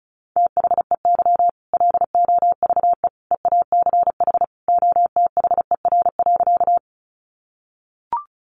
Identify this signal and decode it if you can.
Morse code, heard as a keyed tone, about 34 wpm, 700 Hz. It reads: THEY LOVE EACH OTHER.
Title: Easy Phrases 35wpm